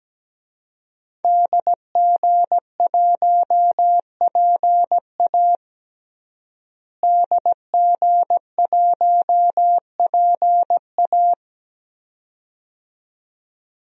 Morse code, heard as a keyed tone, about 17 wpm, 700 Hz. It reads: DG1PA DG1PA
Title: DXCC-100 Call Signs - 5 Characters - Repeated 2x - 17wpm